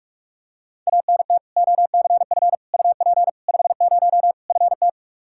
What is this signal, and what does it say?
Morse code, heard as a keyed tone, about 46 wpm, 700 Hz. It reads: ANT OCF UP 50 FT
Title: Practice QSO - Exchange 2 - Info - 45wpm